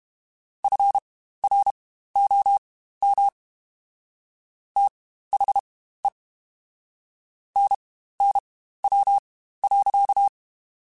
Morse code, heard as a keyed tone, about 32 wpm, spaced as for 15 wpm, 800 Hz. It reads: FROM THE NNW.
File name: Spaced Characters / Spaced Characters-mp3s / Weather 5 12 sp @30